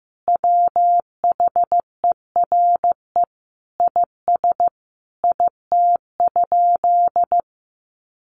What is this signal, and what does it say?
Morse code, heard as a keyed tone, about 15 wpm, 700 Hz. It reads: WHERE IS IT?
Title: Sentences from Top 200 Words 15wpm